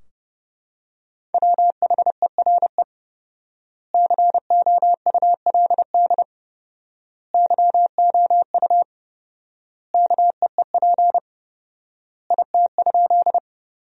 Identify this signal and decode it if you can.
Morse code, heard as a keyed tone, about 30 wpm, 700 Hz. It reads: WHERE COULD YOU KEEP IT?
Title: 4x Word-Spacing - Sentences from Top 300 Words - 30wpm